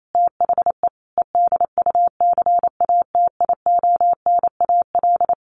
Morse code, heard as a keyed tone, about 28 wpm, 700 Hz. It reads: THE EDUCATIONAL
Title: Sentences from Top 1200 Words 28wpm